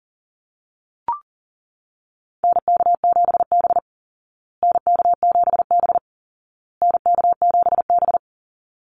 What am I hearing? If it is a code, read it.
Morse code, heard as a keyed tone, about 40 wpm, 700 Hz. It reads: NK7B NK7B NK7B
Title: Call Signs Repeated 2x - 40wpm